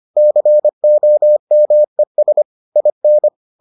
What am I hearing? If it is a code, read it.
Morse code, heard as a keyed tone, about 25 wpm, 600 Hz. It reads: COMES IN